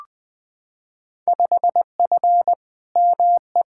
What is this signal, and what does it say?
Morse code, heard as a keyed tone, about 20 wpm, 700 Hz. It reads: 5F ME
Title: Field Day Exchanges Encore - 20wpm